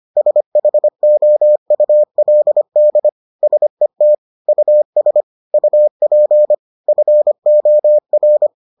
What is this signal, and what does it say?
Morse code, heard as a keyed tone, about 25 wpm, 600 Hz. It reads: SHOULD SET US UP FOR